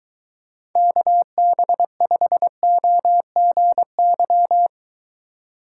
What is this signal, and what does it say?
Morse code, heard as a keyed tone, about 22 wpm, 700 Hz. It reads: KB5OGY